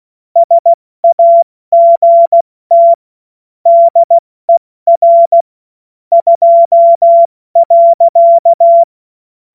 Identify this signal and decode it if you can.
Morse code, heard as a keyed tone, about 16 wpm, 680 Hz. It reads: SAGT DER 2.
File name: Morsekurs / Witze / Witz 02 April 2024 70 F80